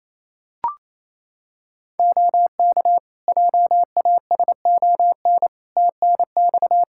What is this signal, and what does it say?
Morse code, heard as a keyed tone, about 27 wpm, 700 Hz. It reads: OK JASON TNX
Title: Practice QSO - Exchange 4 - Ending - 28wpm